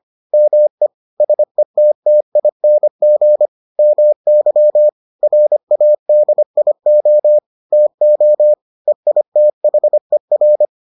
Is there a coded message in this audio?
Yes: Morse code from 0.3 to 10.7 s, heard as a keyed tone, about 25 wpm, 600 Hz. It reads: ME SETTING MY RADIO TO EITHER